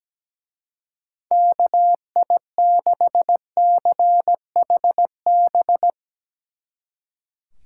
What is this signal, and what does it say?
Morse code, heard as a keyed tone, about 17 wpm, 700 Hz. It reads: KI6CHB
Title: Speed Racing - USA 2x3 Call Signs 17wpm